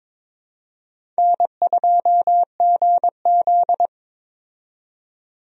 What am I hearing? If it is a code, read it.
Morse code, heard as a keyed tone, about 23 wpm, 700 Hz. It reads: N2GZ
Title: USA Call Signs - 4 Characters - 22wpm